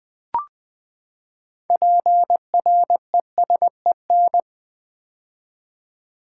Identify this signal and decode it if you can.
Morse code, heard as a keyed tone, about 20 wpm, 700 Hz. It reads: PRESEN
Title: Build A Word - 6 Letters At A Time - 12-Letter Words - 20wpm